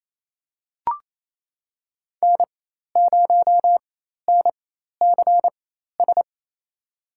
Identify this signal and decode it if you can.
Morse code, heard as a keyed tone, about 27 wpm, spaced as for 13 wpm, 700 Hz. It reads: N0NCS
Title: USA Call Signs - 5 Characters - 4x Character Spacing - 28wpm